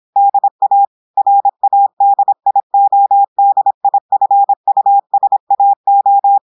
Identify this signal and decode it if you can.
Morse code, heard as a keyed tone, about 26 wpm, 800 Hz. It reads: DA RADIODIFUSAO
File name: audios / portugais / portugais texte en clair 1 vitesse 26 mots minutes